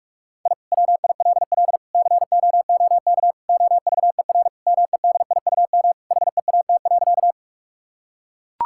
Morse code, heard as a keyed tone, about 41 wpm, 700 Hz. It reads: I WILL COOK OVER MEDIUM HEAT.
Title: Sentences from Top 1000 Words - Review Part 6 - 45wpm